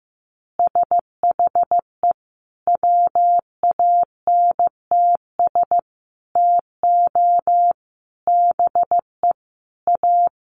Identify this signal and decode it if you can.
Morse code, heard as a keyed tone, about 15 wpm, 700 Hz. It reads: SHE WANTS TO BE A
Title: Easy Phrases 15wpm